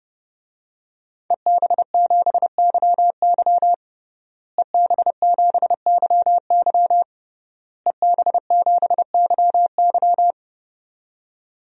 Morse code, heard as a keyed tone, about 30 wpm, 700 Hz. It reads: EB7YY EB7YY EB7YY
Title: DXCC-100 Call Signs - 5 Characters - Repeated 3x - 30wpm